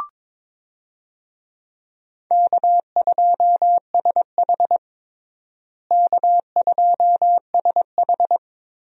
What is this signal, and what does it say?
Morse code, heard as a keyed tone, about 22 wpm, 700 Hz. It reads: K2SH K2SH
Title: USA Call Signs - 4 Characters - Repeated 3x - 22wpm